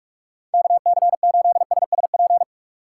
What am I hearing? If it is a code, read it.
Morse code, heard as a keyed tone, about 46 wpm, 700 Hz. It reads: KC8SSP
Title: USA Call Signs - 3x Character Spacing - 45wpm